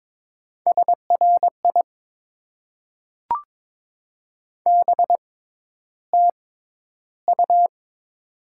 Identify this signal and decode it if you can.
Morse code, heard as a keyed tone, about 22 wpm, spaced as for 11 wpm, 700 Hz. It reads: SRI BTU
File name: Top 90 QSO Elements - 6x Character Spacing - 22wpm